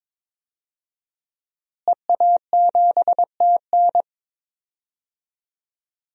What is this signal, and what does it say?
Morse code, heard as a keyed tone, about 22 wpm, 700 Hz. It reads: EA7TN
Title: DXCC-100 Call Signs - 5 Characters - Repeated 2x - 22wpm